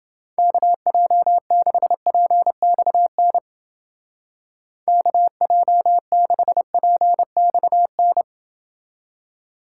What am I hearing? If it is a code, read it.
Morse code, heard as a keyed tone, about 29 wpm, 700 Hz. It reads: KJ6PXN KJ6PXN